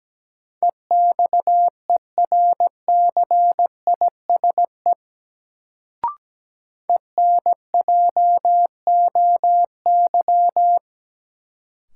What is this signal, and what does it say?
Morse code, heard as a keyed tone, about 17 wpm, 700 Hz. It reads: EXERCISE ENJOY
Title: Sentences from Top 1000 Words - Review Part 4 - New Word Drill - 17wpm